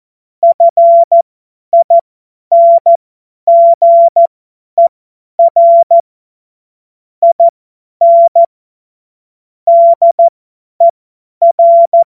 Morse code, heard as a keyed tone, about 14 wpm, spaced as for 10 wpm, 680 Hz. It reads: FINGER IN DER
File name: Morsekurs / Witze / Witz 12. April 2022 50 F70